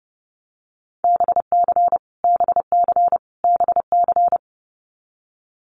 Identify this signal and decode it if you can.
Morse code, heard as a keyed tone, about 30 wpm, 700 Hz. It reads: BC BC BC
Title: Canadian Provinces and Territories - Repeated 3x - 30wpm